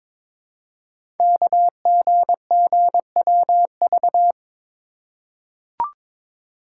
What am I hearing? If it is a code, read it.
Morse code, heard as a keyed tone, about 22 wpm, 700 Hz. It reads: KGGWV
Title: Code Groups - 5 Letters - 22wpm